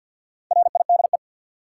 Morse code, heard as a keyed tone, about 53 wpm, 700 Hz. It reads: RIDE
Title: Sentences from Top 1000 Words - Review Part 6 - New Word Drill - 50wpm